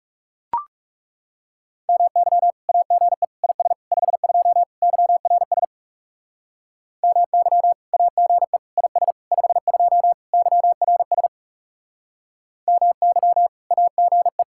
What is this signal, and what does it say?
Morse code, heard as a keyed tone, about 39 wpm, 700 Hz. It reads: MY AGE IS 52 YRS MY AGE IS 52 YRS MY AGE